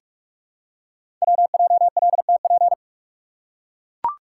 Morse code, heard as a keyed tone, about 44 wpm, 700 Hz. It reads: WJLTP
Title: Code Groups - 5 Letters - 45wpm